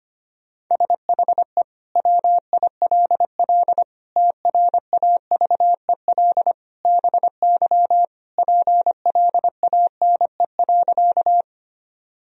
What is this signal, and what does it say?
Morse code, heard as a keyed tone, about 25 wpm, 700 Hz. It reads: SHE WILL TRAVEL BY PLANE.
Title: Sentences from Top 1000 Words - Review Part 3 - 25wpm